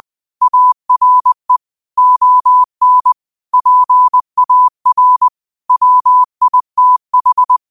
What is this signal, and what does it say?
Morse code, heard as a keyed tone, about 20 wpm, 1000 Hz. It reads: ARE ON PAR WITH
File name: qotdfiles / 20wpm